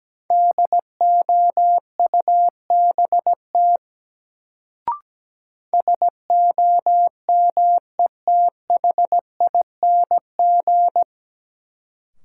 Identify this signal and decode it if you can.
Morse code, heard as a keyed tone, about 17 wpm, 700 Hz. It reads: DOUBT SOMETHING
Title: Sets of 5 Words - New Word Drill - 17wpm